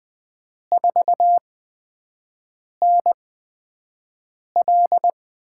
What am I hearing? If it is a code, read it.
Morse code, heard as a keyed tone, about 20 wpm, spaced as for 5 wpm, 700 Hz. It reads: 4NL
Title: USA Call Signs - 4 Characters - 8x Character Spacing - 20wpm